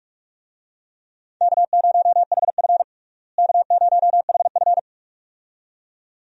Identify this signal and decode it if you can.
Morse code, heard as a keyed tone, about 46 wpm, 700 Hz. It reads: K0HF K0HF
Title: USA Call Signs - 4 Characters - Repeated 2x - 45wpm